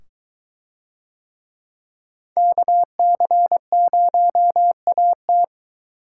Morse code, heard as a keyed tone, about 22 wpm, 700 Hz. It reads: KC0AT